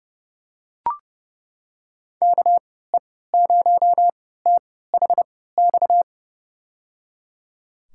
Morse code, heard as a keyed tone, about 30 wpm, spaced as for 17 wpm, 700 Hz. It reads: KE0THX
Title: USA Call Signs - 3x Character Spacing - 30wpm